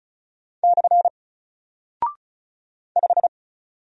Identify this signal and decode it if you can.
Morse code, heard as a keyed tone, about 36 wpm, 700 Hz. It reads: / 5
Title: Single Character (Letter:Number:Punctuation:Prosign) 35wpm